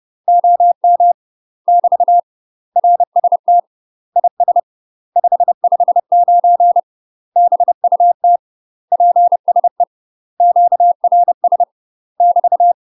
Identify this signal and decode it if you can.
Morse code, heard as a keyed tone, about 30 wpm, 700 Hz. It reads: OM <BT> RST IS 559 BUT PSE QRS <BT>
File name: UK MASTER QSOs TRAINING/30WPM